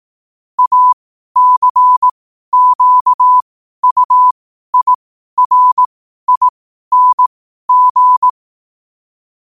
Morse code, heard as a keyed tone, about 18 wpm, spaced as for 12 wpm, 1000 Hz. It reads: ACQUIRING
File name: 12wpm